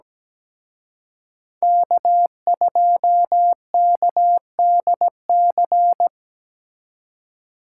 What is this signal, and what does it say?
Morse code, heard as a keyed tone, about 20 wpm, 700 Hz. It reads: K2KDC
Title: Speed Racing - USA 1x3 Call Signs 17wpm